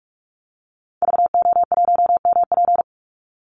Morse code, heard as a keyed tone, about 44 wpm, 700 Hz. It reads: VO1MP
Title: Call Signs Repeated 2x - 45wpm